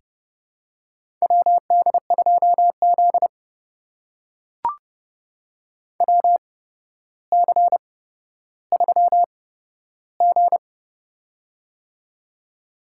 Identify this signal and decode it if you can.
Morse code, heard as a keyed tone, about 30 wpm, spaced as for 13 wpm, 700 Hz. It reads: WD2Z WC3G
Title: USA Call Signs - 4 Characters - 8x Character Spacing - 30wpm